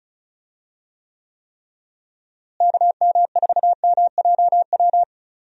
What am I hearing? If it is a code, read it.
Morse code, heard as a keyed tone, about 34 wpm, 700 Hz. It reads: KM4MJW
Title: USA Call Signs - 6 Characters - Repeated 3x - 35wpm